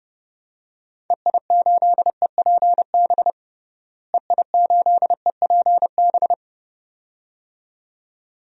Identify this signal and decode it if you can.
Morse code, heard as a keyed tone, about 30 wpm, 700 Hz. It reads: EI8EPB EI8EPB
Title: DXCC-100 Call Signs - 6 Characters - Repeated 2x - 30wpm